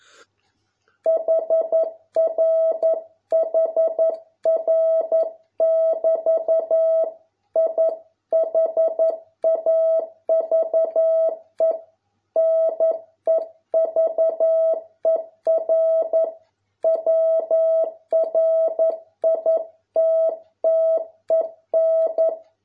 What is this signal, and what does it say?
Morse code, heard as a keyed tone, about 11 wpm, 650 Hz. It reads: HR HR <BT> I HAVE NEVER WRITTEN